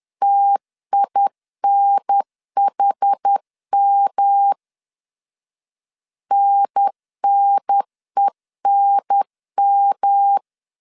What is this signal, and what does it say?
Morse code, heard as a keyed tone, about 10 wpm, 800 Hz. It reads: TINHM NNENM
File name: 10wpm